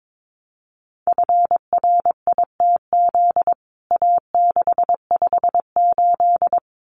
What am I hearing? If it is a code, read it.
Morse code, heard as a keyed tone, about 22 wpm, 700 Hz. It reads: FRITZ A658
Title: CWT Exchange 22wpm